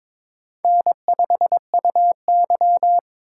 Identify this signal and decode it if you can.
Morse code, heard as a keyed tone, about 22 wpm, 700 Hz. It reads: N5UY
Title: USA Call Signs - 4 Characters - 3x Character Spacing - 22wpm